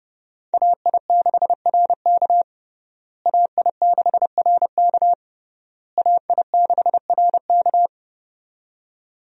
Morse code, heard as a keyed tone, about 30 wpm, 700 Hz. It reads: AI6RK AI6RK AI6RK
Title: USA Call Signs - 5 Characters - Repeated 3x - 30wpm